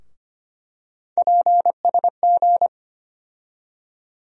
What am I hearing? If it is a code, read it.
Morse code, heard as a keyed tone, about 25 wpm, 700 Hz. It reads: PSG
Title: USA Call Sign Postfixes - 3 Letters 25wpm